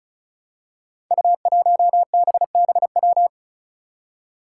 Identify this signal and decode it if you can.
Morse code, heard as a keyed tone, about 36 wpm, 700 Hz. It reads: U1BBW
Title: Five Challenging Letter-Number Combinations 35wpm